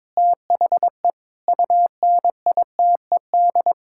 Morse code, heard as a keyed tone, about 22 wpm, 700 Hz. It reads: THE UNITED
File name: practice / 22wpm / Sets of 8 Words - 22wpm